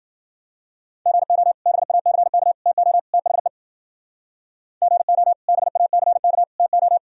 Morse code, heard as a keyed tone, about 49 wpm, 700 Hz. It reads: GO BACK TO THE GO BACK TO
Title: Speed-Racing - Sets of 4 Words 40wpm